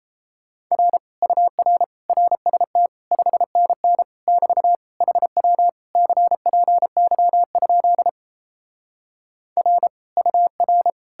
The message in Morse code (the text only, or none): R UR RST 5NN <BT> HW CPY? R UR